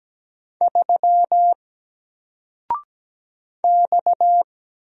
Morse code, heard as a keyed tone, about 17 wpm, 700 Hz. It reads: 3 X
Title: Single Letter-Number 17wpm